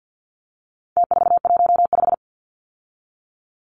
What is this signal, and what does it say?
Morse code, heard as a keyed tone, about 50 wpm, 700 Hz. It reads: T415
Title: Numbers - Contest Style - 4-Digit 50wpm